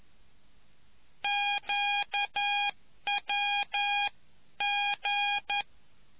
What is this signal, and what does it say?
Morse code, heard as a keyed tone, about 11 wpm, spaced as for 9 wpm, 810 Hz. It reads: QWG